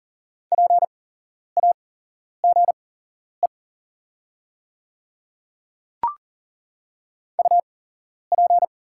Morse code, heard as a keyed tone, about 37 wpm, spaced as for 11 wpm, 700 Hz. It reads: PAGE UP
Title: 8x Character Spacing - Top 300 Words - 40wpm